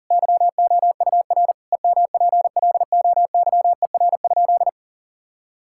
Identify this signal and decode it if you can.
Morse code, heard as a keyed tone, about 40 wpm, 700 Hz. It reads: YOUR EMPLOYER?
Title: Sentences from Top 1600 Words 40wpm